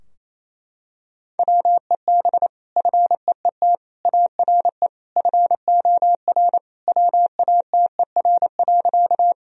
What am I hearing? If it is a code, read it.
Morse code, heard as a keyed tone, about 28 wpm, 700 Hz. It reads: WEB FEET ARE FOR WATER.
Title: Sentences from Top 1000 Words - Review Part 4 - 28wpm